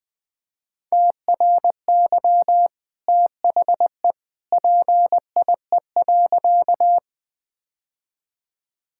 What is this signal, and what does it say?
Morse code, heard as a keyed tone, about 20 wpm, 700 Hz. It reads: TRY THE PIE.
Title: Easy Sentences with Hints - 20wpm